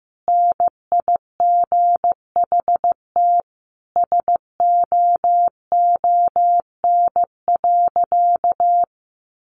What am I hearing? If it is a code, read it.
Morse code, heard as a keyed tone, about 15 wpm, 700 Hz. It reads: NIGHT SOON.
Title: Sentences from Top 1300 Words 15wpm